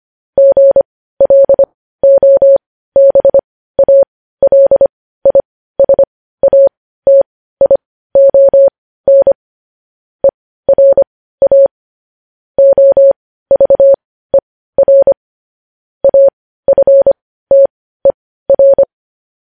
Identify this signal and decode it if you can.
Morse code, heard as a keyed tone, about 25 wpm, spaced as for 15 wpm, 550 Hz. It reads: GLOBALISATION ERA OVER AFTER
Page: news-headlines-in-morse-code-at-15-wpm